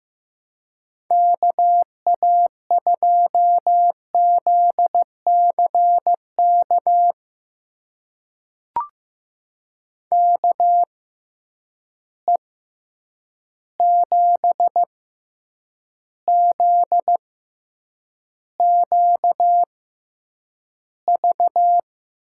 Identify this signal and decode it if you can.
Morse code, heard as a keyed tone, about 15 wpm, spaced as for 8 wpm, 700 Hz. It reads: KA2ZCK KE7ZQV
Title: USA Call Signs - 6 Characters - 6x Character Spacing - 15wpm